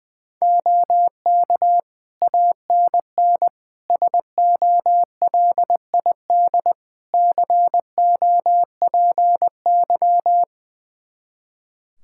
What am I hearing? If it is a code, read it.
Morse code, heard as a keyed tone, about 20 wpm, 700 Hz. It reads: OK ANN SOLID COPY
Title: K1USN SST QSO Practice 20wpm